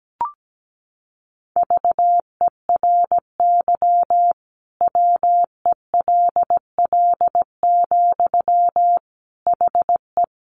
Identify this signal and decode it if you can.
Morse code, heard as a keyed tone, about 17 wpm, 700 Hz. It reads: VERY WELL, HE
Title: Sentences from Top 200 Words 17wpm